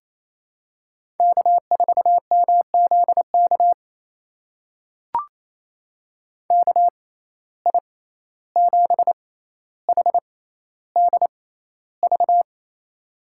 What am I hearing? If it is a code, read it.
Morse code, heard as a keyed tone, about 28 wpm, spaced as for 14 wpm, 700 Hz. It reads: K4MZK KI7HDV
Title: USA Call Signs - 6x Character Spacing - 28wpm